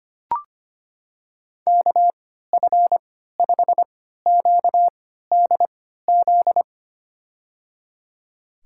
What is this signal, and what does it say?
Morse code, heard as a keyed tone, about 25 wpm, spaced as for 14 wpm, 700 Hz. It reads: KF5QDZ
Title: USA Call Signs - 3x Character Spacing - 25wpm